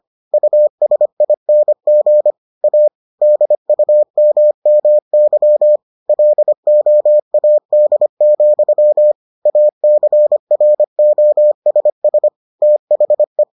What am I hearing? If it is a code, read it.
Morse code, heard as a keyed tone, about 25 wpm, 600 Hz. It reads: USING A DUMMY LOAD, ACROSS THE